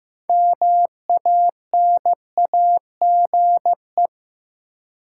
Morse code, heard as a keyed tone, about 15 wpm, 700 Hz. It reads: MANAGE
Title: Sentences from Top 1400 Words - New Word Drill - 15wpm